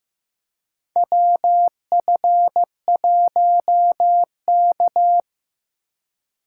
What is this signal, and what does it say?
Morse code, heard as a keyed tone, about 15 wpm, 700 Hz. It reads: WF1K
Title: USA Call Signs - 4x Character Spacing - 15wpm